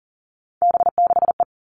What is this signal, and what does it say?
Morse code, heard as a keyed tone, about 40 wpm, 700 Hz. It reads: B6E
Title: All Cut Numbers - 3-Digit 40wpm